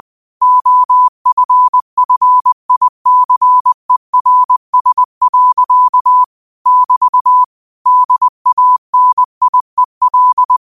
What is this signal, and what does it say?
Morse code, heard as a keyed tone, about 20 wpm, 1000 Hz. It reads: OFFICERS. <BT> DANIEL